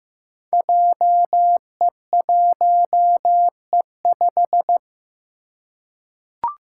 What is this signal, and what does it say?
Morse code, heard as a keyed tone, about 15 wpm, 700 Hz. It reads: JE1E5
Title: Five Challenging Letter-Number Combinations 15wpm